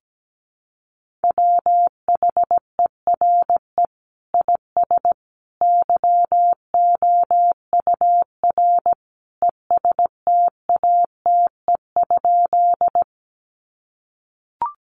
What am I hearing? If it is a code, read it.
Morse code, heard as a keyed tone, about 17 wpm, 700 Hz. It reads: WHERE IS YOUR ESTATE?
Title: Sentences from Top 1200 Words 17wpm